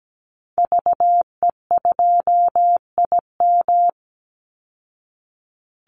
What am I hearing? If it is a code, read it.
Morse code, heard as a keyed tone, about 17 wpm, 700 Hz. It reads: VE2IM
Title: Call Signs Repeated 2x - 17wpm